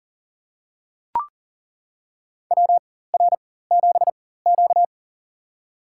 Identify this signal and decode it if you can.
Morse code, heard as a keyed tone, about 40 wpm, spaced as for 19 wpm, 700 Hz. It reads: WR7Q
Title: USA Call Signs - 4 Characters - 4x Character Spacing - 40wpm